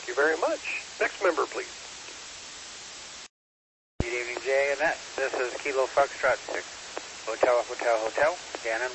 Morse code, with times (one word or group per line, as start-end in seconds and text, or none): none